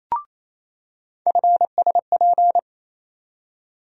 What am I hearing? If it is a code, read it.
Morse code, heard as a keyed tone, about 28 wpm, 700 Hz. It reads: FSP
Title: USA Call Sign Postfixes - 3 Letters 28wpm